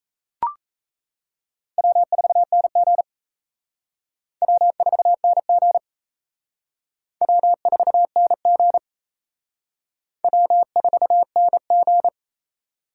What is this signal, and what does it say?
Morse code, heard as a keyed tone, about 33 wpm, 700 Hz. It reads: W4NG W4NG W4NG W4NG
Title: Speed Racing - USA 1x2 Call Signs 28wpm